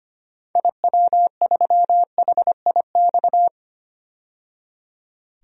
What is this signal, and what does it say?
Morse code, heard as a keyed tone, about 25 wpm, 700 Hz. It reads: IW3HIX
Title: DXCC-100 Call Signs - 6 Characters - 25wpm